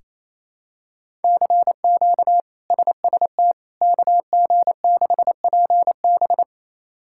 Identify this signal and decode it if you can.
Morse code, heard as a keyed tone, about 28 wpm, 700 Hz. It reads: CQ SST KG6PB
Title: K1USN SST QSO Practice - 28wpm